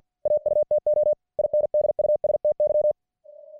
Morse code, heard as a keyed tone, about 50 wpm, 600 Hz. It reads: AUTO INDUSTY